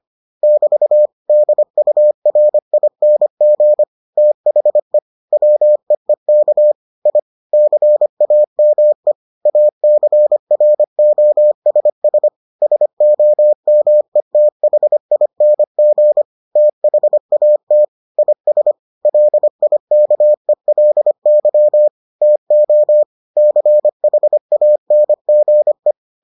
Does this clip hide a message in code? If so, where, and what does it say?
Yes: Morse code from 0.4 to 25.9 s, heard as a keyed tone, about 25 wpm, 600 Hz. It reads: <BT> DURING THE WEEK I CAME ACROSS SOMETHING THAT IS LIKELY TO CHANGE